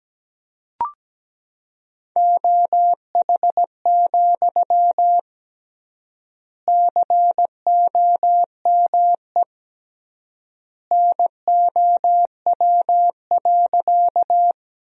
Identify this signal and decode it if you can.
Morse code, heard as a keyed tone, about 17 wpm, 700 Hz. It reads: OH, COME NOW.
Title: 3x Word-Spacing - Sentences from Top 500 Words - 17wpm